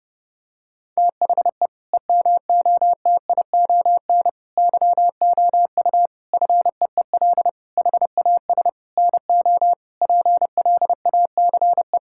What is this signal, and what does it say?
Morse code, heard as a keyed tone, about 32 wpm, 700 Hz. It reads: THE EMOTION YOU FEEL HAS NO PLACE
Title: Sentences from Top 1600 Words 30wpm